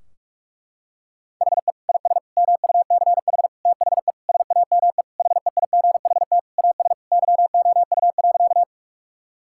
Morse code, heard as a keyed tone, about 41 wpm, 700 Hz. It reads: HE IS MUCH THE SAME HEIGHT AS YOU.
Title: Sentences from Top 1600 Words 45wpm